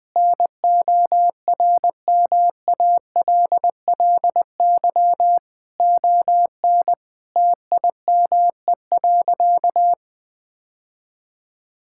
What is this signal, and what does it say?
Morse code, heard as a keyed tone, about 20 wpm, 700 Hz. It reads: NORMALLY ON TIME.
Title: Sentences from Top 1000 Words - Review Part 6 - 20wpm